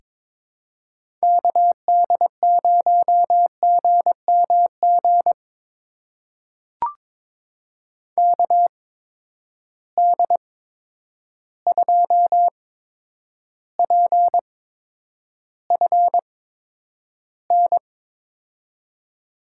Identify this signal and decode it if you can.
Morse code, heard as a keyed tone, about 22 wpm, spaced as for 9 wpm, 700 Hz. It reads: KD0GMG KD2PFN
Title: USA Call Signs - 6 Characters - 8x Character Spacing - 22wpm